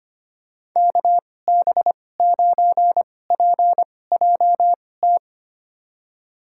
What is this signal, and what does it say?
Morse code, heard as a keyed tone, about 25 wpm, spaced as for 18 wpm, 700 Hz. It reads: KB9PJT
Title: USA Call Signs - 6 Characters - 2x Character Spacing - 25wpm